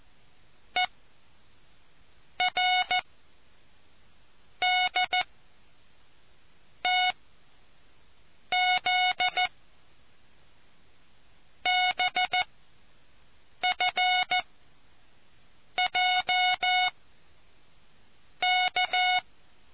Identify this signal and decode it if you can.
Morse code, heard as a keyed tone, about 14 wpm, spaced as for 6 wpm, 730 Hz. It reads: E R D T Z BF J K